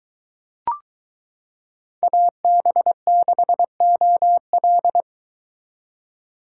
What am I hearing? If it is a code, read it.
Morse code, heard as a keyed tone, about 23 wpm, 700 Hz. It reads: AB6OL